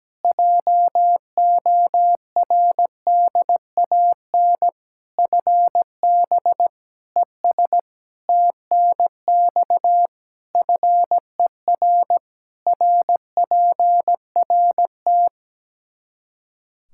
Morse code, heard as a keyed tone, about 17 wpm, 700 Hz. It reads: JORDAN FB ES TNX FER RPRT
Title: Practice QSO - Exchange 2 - Info - 17wpm